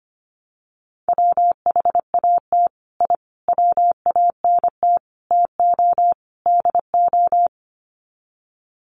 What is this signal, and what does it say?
Morse code, heard as a keyed tone, about 25 wpm, 700 Hz. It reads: WHAT I WANT TO DO
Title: Sets of 5 Words Encore 25wpm